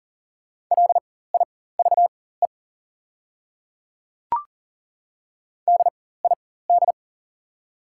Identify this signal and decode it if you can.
Morse code, heard as a keyed tone, about 40 wpm, spaced as for 19 wpm, 690 Hz. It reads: LIVE DID
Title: 4x Character Spacing - Top 500 Words - 40wpm